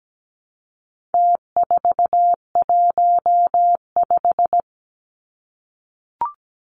Morse code, heard as a keyed tone, about 17 wpm, 700 Hz. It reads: T415
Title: Numbers - Contest Style - 4-Digit 17wpm